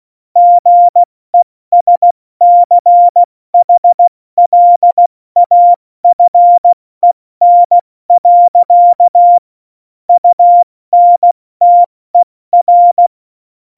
Morse code, heard as a keyed tone, about 16 wpm, spaced as for 14 wpm, 700 Hz. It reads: GESCHLAFEN. UNTER
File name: Morsekurs / Witze / Witz 01. September 2020, 70 F80